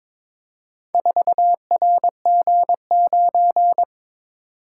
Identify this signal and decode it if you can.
Morse code, heard as a keyed tone, about 22 wpm, 700 Hz. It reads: 4RG9
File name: practice / 22wpm / Four Letter-Number Combinations - 22wpm